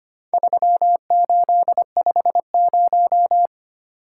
Morse code, heard as a keyed tone, about 25 wpm, 700 Hz. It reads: 3850
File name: practice / 25wpm / Four Letter-Number Combinations - 25wpm